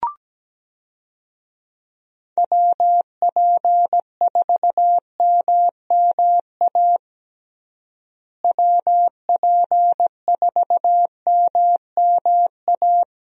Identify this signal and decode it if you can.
Morse code, heard as a keyed tone, about 17 wpm, 700 Hz. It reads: WP4MMA WP4MMA
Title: USA Call Signs - 6 Characters - Repeated 2x - 17wpm